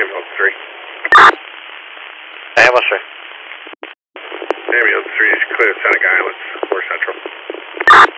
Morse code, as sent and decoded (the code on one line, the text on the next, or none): none